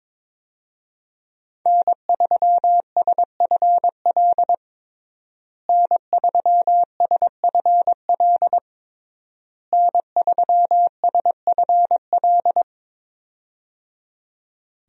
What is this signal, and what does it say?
Morse code, heard as a keyed tone, about 22 wpm, 700 Hz. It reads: N3SFL N3SFL N3SFL
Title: USA Call Signs - 5 Characters - Repeated 3x - 22wpm